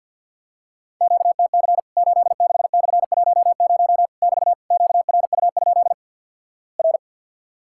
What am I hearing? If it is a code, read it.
Morse code, heard as a keyed tone, about 50 wpm, 700 Hz. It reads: QTC 86/10 <BT> QRU? R
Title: Worked All Europe DX Contest - QTC Exchange - 50wpm